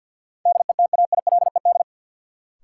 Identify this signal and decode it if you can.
Morse code, heard as a keyed tone, about 53 wpm, 700 Hz. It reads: DETAILED
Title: Sentences from Top 1500 Words - New Word Drill - 50wpm